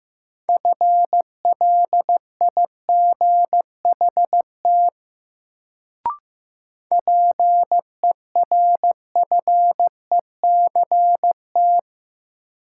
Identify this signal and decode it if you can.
Morse code, heard as a keyed tone, about 15 wpm, 700 Hz. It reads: FLIGHT PERFECT
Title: Sentences from Top 1000 Words - New Word Drill - 15wpm